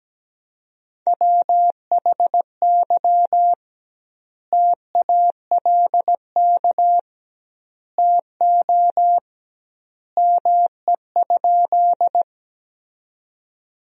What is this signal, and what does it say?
Morse code, heard as a keyed tone, about 17 wpm, 700 Hz. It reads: WHY TALK TO ME?
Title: 2x Word-Spacing - Sentences from Top 400 Words - 17wpm